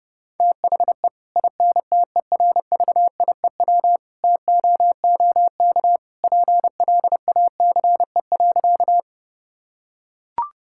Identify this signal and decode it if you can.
Morse code, heard as a keyed tone, about 30 wpm, 700 Hz. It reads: THE INTERVIEW TOOK PLACE.
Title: Sentences from Top 1600 Words 30wpm